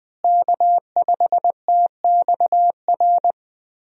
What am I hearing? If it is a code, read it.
Morse code, heard as a keyed tone, about 20 wpm, 700 Hz. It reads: K5TXR
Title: USA Call Signs - 5 Characters - Repeated 2x - 20wpm